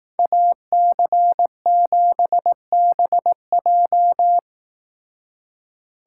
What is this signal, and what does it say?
Morse code, heard as a keyed tone, about 18 wpm, 700 Hz. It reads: AC7BJ
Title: Speed Racing - USA Call Signs 15wpm